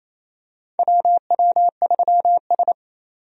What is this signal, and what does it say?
Morse code, heard as a keyed tone, about 28 wpm, 690 Hz. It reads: WW3S
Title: USA Call Signs - 4 Characters - 28wpm